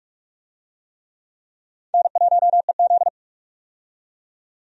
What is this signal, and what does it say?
Morse code, heard as a keyed tone, about 46 wpm, 700 Hz. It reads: N1EZ
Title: USA Call Signs - 4 Characters - 45wpm